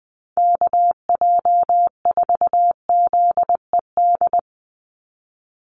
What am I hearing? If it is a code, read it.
Morse code, heard as a keyed tone, about 20 wpm, 700 Hz. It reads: KJ4ZED
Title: USA Call Signs 20wpm